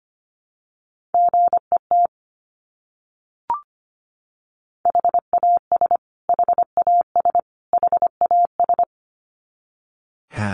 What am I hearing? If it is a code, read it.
Morse code, heard as a keyed tone, about 25 wpm, 700 Hz. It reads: GET HAS HAS HAS
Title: Top 200 Words Repeated 25wpm